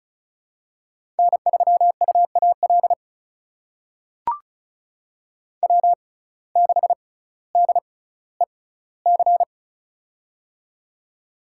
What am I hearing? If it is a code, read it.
Morse code, heard as a keyed tone, about 36 wpm, spaced as for 18 wpm, 700 Hz. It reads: N3UAL W6DEC
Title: USA Call Signs - 6x Character Spacing - 35wpm